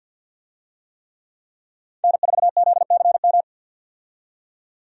Morse code, heard as a keyed tone, about 51 wpm, 700 Hz. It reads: N4ZCM
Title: USA Call Signs - 5 Characters - 50wpm